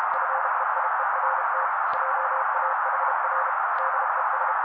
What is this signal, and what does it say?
Morse code, heard as a keyed tone, about 32 wpm, 530 Hz. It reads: FEIERT JAHR DES
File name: daily / 31wpm noise